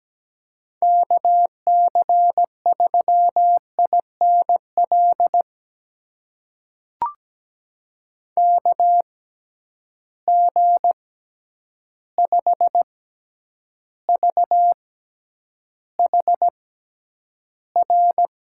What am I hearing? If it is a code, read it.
Morse code, heard as a keyed tone, about 17 wpm, spaced as for 9 wpm, 700 Hz. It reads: KC3INL KG5VHR
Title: USA Call Signs - 6 Characters - 6x Character Spacing - 17wpm